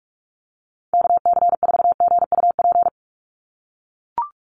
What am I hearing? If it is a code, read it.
Morse code, heard as a keyed tone, about 44 wpm, 700 Hz. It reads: KC4GUP
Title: USA Call Signs 45wpm